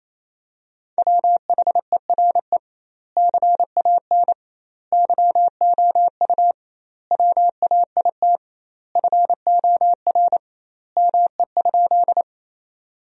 Morse code, heard as a keyed tone, about 28 wpm, 700 Hz. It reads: WHERE CAN YOU WAIT FOR ME?
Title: 2x Word-Spacing - Sentences from Top 500 Words - 28wpm